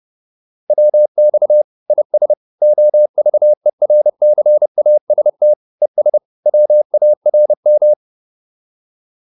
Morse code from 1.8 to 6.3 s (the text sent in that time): IS OVERCAST ES